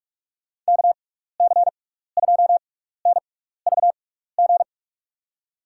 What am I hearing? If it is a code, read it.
Morse code, heard as a keyed tone, about 46 wpm, 700 Hz. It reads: K C 2 N V G
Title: USA Call Signs - 6 Characters - 6x Character Spacing - 45wpm